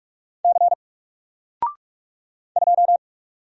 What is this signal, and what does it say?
Morse code, heard as a keyed tone, about 44 wpm, 700 Hz. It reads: C 2
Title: Single Letter-Number Alt 45wpm